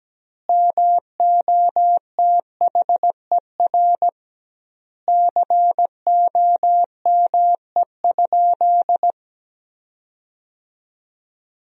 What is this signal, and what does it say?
Morse code, heard as a keyed tone, about 17 wpm, 700 Hz. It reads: MOTHER COME?
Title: 2x Word-Spacing - Sentences from Top 200 Words - 17wpm